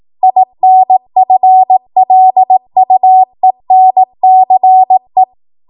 Morse code, heard as a keyed tone, about 18 wpm, 750 Hz. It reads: INFLUENCE
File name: Morse / 18WPM